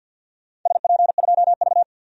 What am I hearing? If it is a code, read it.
Morse code, heard as a keyed tone, about 50 wpm, 700 Hz. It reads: SP2V
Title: DXCC-100 Call Signs - 4 Characters - 50wpm